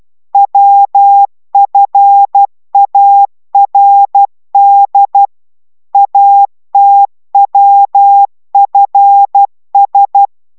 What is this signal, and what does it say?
Morse code, heard as a keyed tone, about 12 wpm, 800 Hz. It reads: WFARD ATWFS